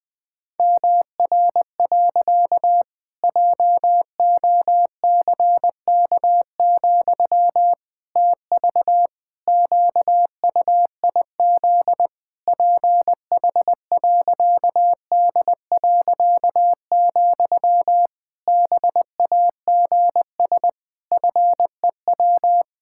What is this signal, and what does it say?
Morse code, heard as a keyed tone, about 20 wpm, 700 Hz. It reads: MR. JOCK, TV QUIZ PH.D., BAGS FEW
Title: Pangram Challenge v2 - 20wpm